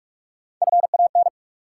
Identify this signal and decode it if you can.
Morse code, heard as a keyed tone, about 41 wpm, 700 Hz. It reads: FAN
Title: Sentences from Top 1000 Words - Review Part 4 - New Word Drill - 45wpm